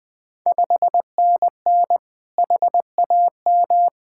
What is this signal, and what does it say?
Morse code, heard as a keyed tone, about 20 wpm, 700 Hz. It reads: 5NN HAM
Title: Ontario QSO Party Exchange 20wpm